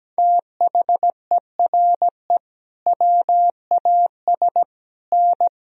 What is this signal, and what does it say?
Morse code, heard as a keyed tone, about 17 wpm, 700 Hz. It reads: THERE WAS N
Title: Sentences from Top 300 Words and Abbreviations 17wpm